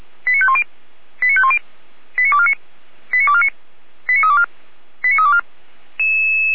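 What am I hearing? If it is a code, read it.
Five-tone selective call (ZVEI), heard as quick runs of five tones, tones of about 70 ms. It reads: 89510 89510 89249 89249 89324 89324
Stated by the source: ZVEI five-tone call received on SDR, from fire brigade Feuerwehr Orlamünde (f85360).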